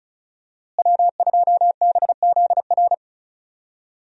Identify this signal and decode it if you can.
Morse code, heard as a keyed tone, about 36 wpm, 700 Hz. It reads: W2BZR
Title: USA Call Signs - 5 Characters - Repeated 3x - 35wpm